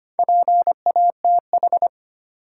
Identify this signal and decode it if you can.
Morse code, heard as a keyed tone, about 25 wpm, 700 Hz. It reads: PATH
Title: Sentences from Top 1400 Words - New Word Drill - 25wpm